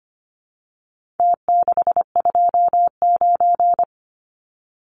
Morse code, heard as a keyed tone, about 25 wpm, 700 Hz. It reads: T629
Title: Numbers - Contest Style - 4-Digit 25wpm